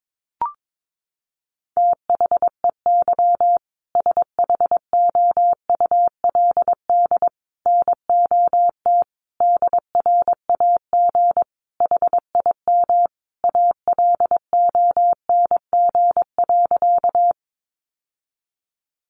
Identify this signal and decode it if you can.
Morse code, heard as a keyed tone, about 22 wpm, 700 Hz. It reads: THEY SHOULD NOT DRAG HIM ALONG.
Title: Sentences from Top 1500 Words 22wpm